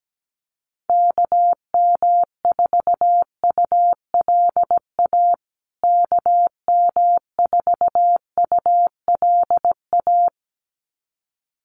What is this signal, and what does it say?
Morse code, heard as a keyed tone, about 17 wpm, 700 Hz. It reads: KM4ULA KM4ULA
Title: USA Call Signs Repeated 2x 17wpm